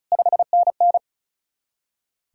Morse code, heard as a keyed tone, about 44 wpm, 700 Hz. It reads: 5NN